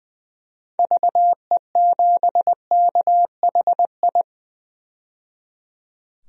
Morse code, heard as a keyed tone, about 20 wpm, 700 Hz. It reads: VE7KHI
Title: DXCC-100 Call Signs - 6 Characters - 20wpm